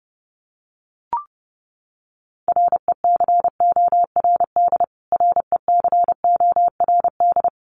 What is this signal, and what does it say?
Morse code, heard as a keyed tone, about 30 wpm, 700 Hz. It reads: RECORD RECORD
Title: Top 500 Words Repeated 30wpm